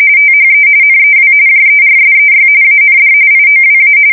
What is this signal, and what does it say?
RTTY, about 45.45 baud. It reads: THE NEW JERSEY AND IOWA
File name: fsk170 / USS-New-Jersey